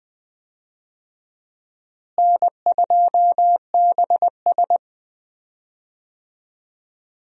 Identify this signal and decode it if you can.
Morse code, heard as a keyed tone, about 20 wpm, 700 Hz. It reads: N2BS
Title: USA Call Signs - 4 Characters - Repeated 3x - 20wpm